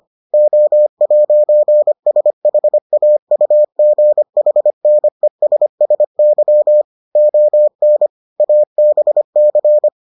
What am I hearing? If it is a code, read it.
Morse code, heard as a keyed tone, about 25 wpm, 600 Hz. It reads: O'SHAUGHNESSY ON ABC